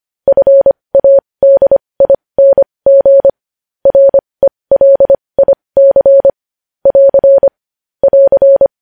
Morse code, heard as a keyed tone, about 25 wpm, spaced as for 20 wpm, 550 Hz. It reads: FADING RELIC <AR> <AR>
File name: podcast / News headlines-1x-25x20